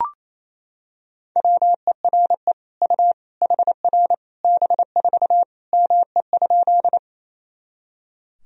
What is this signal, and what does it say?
Morse code, heard as a keyed tone, about 28 wpm, 700 Hz. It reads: WERE U HR B4 ME?
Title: Sentences from Top 200 Words and Abbreviations 28wpm